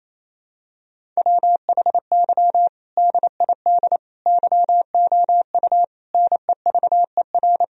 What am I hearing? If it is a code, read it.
Morse code, heard as a keyed tone, about 28 wpm, 700 Hz. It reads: WHY DID YOU NEVER
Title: Sentences from Top 1000 Words - Review Part 1 - 28wpm